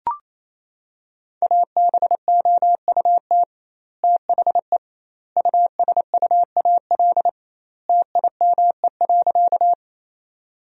Morse code, heard as a keyed tone, about 28 wpm, 700 Hz. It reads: ABOUT THE USUAL TIME.
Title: 2x Word-Spacing - Sentences from Top 400 Words - 28wpm